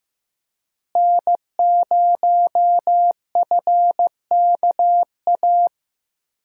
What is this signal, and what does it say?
Morse code, heard as a keyed tone, about 15 wpm, 700 Hz. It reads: N0FKA
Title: USA Call Signs - 5 Characters - Repeated 3x - 15wpm